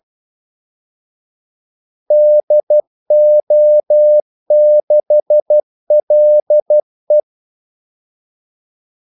Morse code, heard as a keyed tone, about 12 wpm, 600 Hz. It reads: DO6LE
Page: 2 ekstra ordmellemrum mellem kaldesignalerne